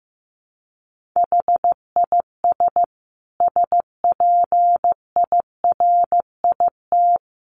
Morse code, heard as a keyed tone, about 15 wpm, 700 Hz. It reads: HIS SPIRIT
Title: Sentences from Top 1400 Words 15wpm